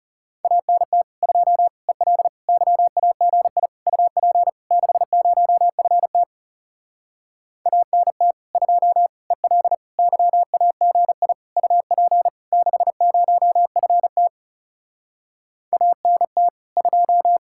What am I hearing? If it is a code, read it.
Morse code, heard as a keyed tone, about 40 wpm, 700 Hz. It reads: ANT 2 EL YAGI UP 60FT ANT 2 EL YAGI UP 60FT ANT 2